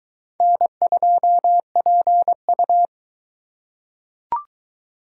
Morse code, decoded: N2PU

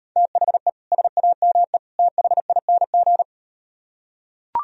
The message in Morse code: THE SAME THING

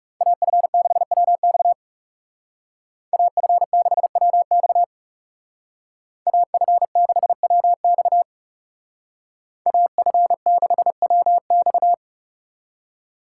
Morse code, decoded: AF6WX AF6WX AF6WX AF6WX

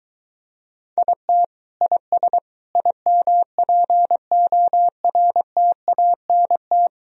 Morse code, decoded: IT IS IMPORTANT

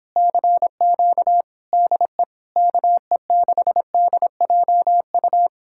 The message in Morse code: CQ DE KE6DJU

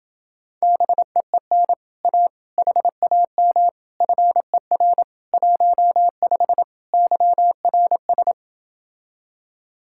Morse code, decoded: BEEN A HAM FER 15 YRS

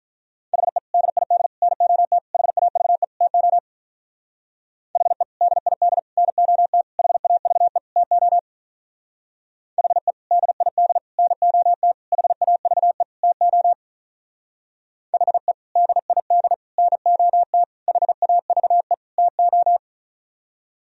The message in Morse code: HE DID NOT HAVE TO HE DID NOT HAVE TO HE DID NOT HAVE TO HE DID NOT HAVE TO

